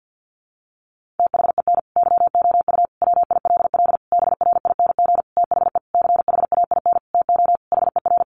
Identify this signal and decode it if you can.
Morse code, heard as a keyed tone, about 50 wpm, 700 Hz. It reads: THEN YOU WILL BRING THE CHAIN TO HER
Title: Sentences from Top 1500 Words 50wpm